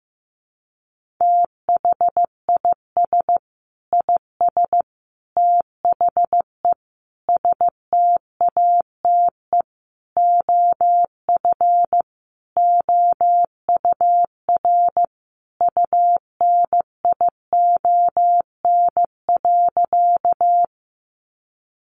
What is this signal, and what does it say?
Morse code, heard as a keyed tone, about 15 wpm, 700 Hz. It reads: THIS IS THE STATE OF OUR UNION.
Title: Sentences from Top 1400 Words 15wpm